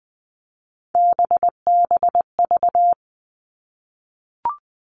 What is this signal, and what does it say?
Morse code, heard as a keyed tone, about 20 wpm, 700 Hz. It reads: BBV
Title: All Cut Numbers - 3-Digit 20wpm